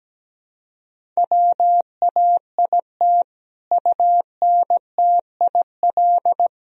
Morse code, heard as a keyed tone, about 17 wpm, 700 Hz. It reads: WAIT UNTIL
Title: Sentences from Top 1000 Words - Review Part 1 - 17wpm